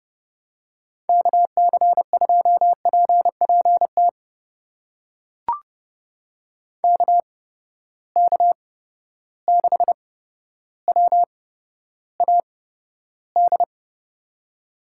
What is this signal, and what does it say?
Morse code, heard as a keyed tone, about 31 wpm, spaced as for 13 wpm, 700 Hz. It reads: KC2PPT KK6WAD
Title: USA Call Signs - 6 Characters - 8x Character Spacing - 30wpm